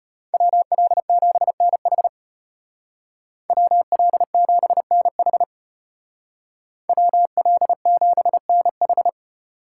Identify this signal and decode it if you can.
Morse code, heard as a keyed tone, about 33 wpm, 700 Hz. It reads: WL7NH WL7NH WL7NH